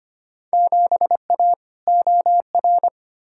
Morse code, heard as a keyed tone, about 25 wpm, 700 Hz. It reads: 7A OR
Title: Field Day Exchanges Encore - 25wpm